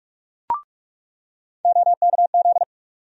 Morse code, heard as a keyed tone, about 46 wpm, 700 Hz. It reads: OKZ
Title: USA Call Sign Postfixes - 3 Letters 45wpm